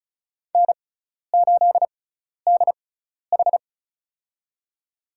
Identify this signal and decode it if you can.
Morse code, heard as a keyed tone, about 34 wpm, spaced as for 12 wpm, 700 Hz. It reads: N8DH
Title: USA Call Signs - 6x Character Spacing - 35wpm